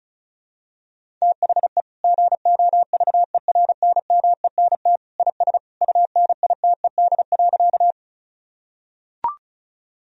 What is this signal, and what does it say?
Morse code, heard as a keyed tone, about 34 wpm, 700 Hz. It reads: THE GOVERNMENT IS UNITED.
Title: Sentences from Top 1000 Words - Review Part 4 - 35wpm